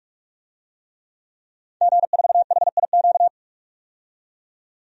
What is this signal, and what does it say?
Morse code, heard as a keyed tone, about 46 wpm, 700 Hz. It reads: G4HIQ
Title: DXCC-100 Call Signs - 5 Characters - 45wpm